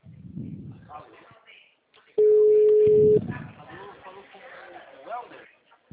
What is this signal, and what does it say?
Ringback tone (Europe, 425 Hz), with speech.
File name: out- -2010- - - .46301